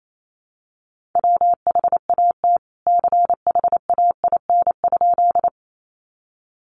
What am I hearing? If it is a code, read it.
Morse code, heard as a keyed tone, about 28 wpm, 700 Hz. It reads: WHAT CHAIN?
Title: Sentences from Top 1500 Words 28wpm